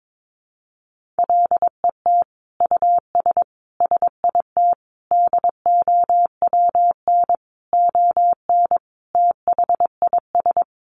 Morse code, heard as a keyed tone, about 22 wpm, 700 Hz. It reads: LET US SIT DOWN ON THIS
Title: Sentences from Top 1400 Words 22wpm